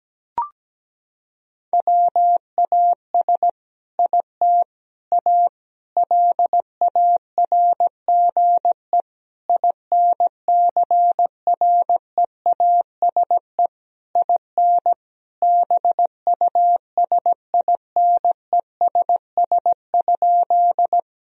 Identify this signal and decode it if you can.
Morse code, heard as a keyed tone, about 17 wpm, 700 Hz. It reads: WAS IT A LARGE INCREASE IN BUSINESS?
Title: Sentences from Top 1000 Words - Review Part 4 - 17wpm